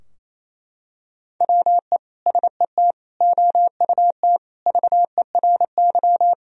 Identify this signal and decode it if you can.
Morse code, heard as a keyed tone, about 27 wpm, 700 Hz. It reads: WE SET OUT VERY
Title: Sentences from Top 1000 Words - Review Part 2 - 28wpm